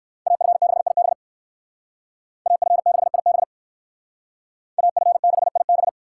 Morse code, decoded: AF6IB AF6IB AF6IB

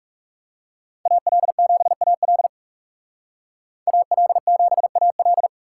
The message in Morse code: AL7AL AL7AL

